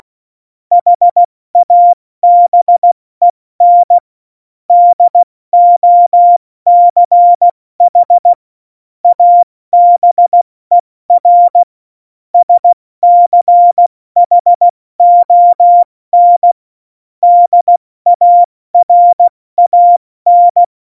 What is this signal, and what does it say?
Morse code, heard as a keyed tone, about 16 wpm, 700 Hz. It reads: HABEN DOCH ABER SCHON DARAN
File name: Morsekurs / Witze / Witz 22. Oktober 2019, 70 F80